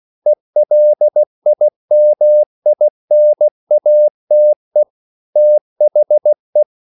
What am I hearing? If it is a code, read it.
Morse code, heard as a keyed tone, about 16 wpm, 600 Hz. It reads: ELIMINATE THE
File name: CW / Practice / EnglishQuotations0216